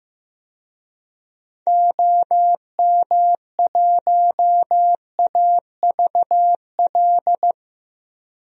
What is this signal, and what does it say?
Morse code, heard as a keyed tone, about 15 wpm, 700 Hz. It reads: OM1AVL
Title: DXCC-100 Call Signs - 6 Characters - 15wpm